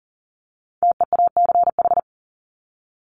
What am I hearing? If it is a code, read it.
Morse code, heard as a keyed tone, about 40 wpm, 700 Hz. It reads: TEACH
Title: Top 500 Words Repeated 40wpm